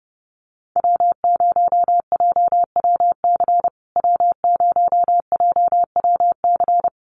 Morse code, heard as a keyed tone, about 30 wpm, 700 Hz. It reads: W0JWC W0JWC
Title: USA Call Signs Repeated 2x 30wpm